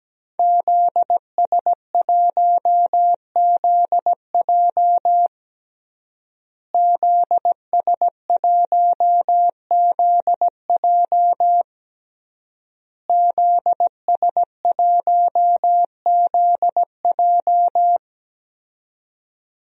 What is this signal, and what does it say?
Morse code, heard as a keyed tone, about 17 wpm, 700 Hz. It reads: ZS1ZJ ZS1ZJ ZS1ZJ
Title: DXCC-100 Call Signs - 5 Characters - Repeated 3x - 17wpm